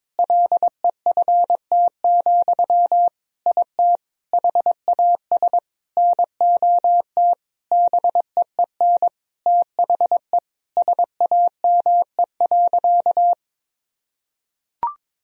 Morse code, heard as a keyed tone, about 22 wpm, 700 Hz. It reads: LEFT, IT HAS NOT BEEN THE SAME.
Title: Sentences from Top 1000 Words - Review Part 1 - 22wpm